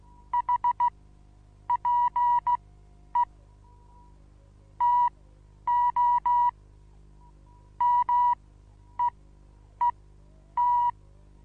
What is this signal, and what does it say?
Morse code, heard as a keyed tone, about 16 wpm, spaced as for 9 wpm, 970 Hz. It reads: HPE TO MEET
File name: Onlinelehrgang / morsen / DJ4UF Morsen25